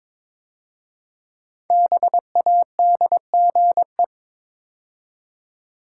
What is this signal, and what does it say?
Morse code, heard as a keyed tone, about 22 wpm, 700 Hz. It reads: BADGE
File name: 5 Letter Words - First 3 Letters Given - 22wpm